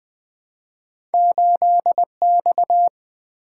Morse code, heard as a keyed tone, about 20 wpm, 700 Hz. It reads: 8X
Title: Two Letter-Number Combinations - 20wpm